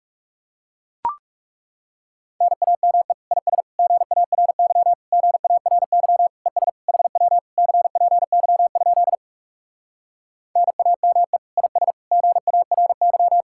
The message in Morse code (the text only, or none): NAME IS GARY GARY ES HW CPY? NAME IS GARY